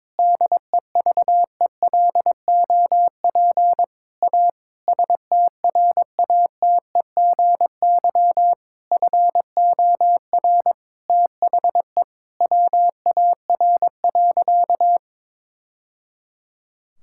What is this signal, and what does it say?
Morse code, heard as a keyed tone, about 22 wpm, 700 Hz. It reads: DEVELOP A STRATEGY FOR THE WAR.
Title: Sentences from Top 1000 Words - Review Part 5 - 22wpm